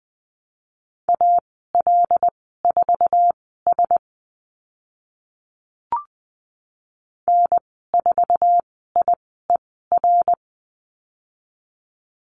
Morse code, heard as a keyed tone, about 20 wpm, spaced as for 15 wpm, 700 Hz. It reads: AL4S N4IER
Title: 082 Lesson 14g - Number 2 - Call Signs 20wpm